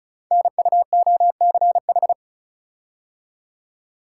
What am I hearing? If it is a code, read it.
Morse code, heard as a keyed tone, about 36 wpm, 700 Hz. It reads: NUOCH
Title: Code Groups - 5 Letters - 35wpm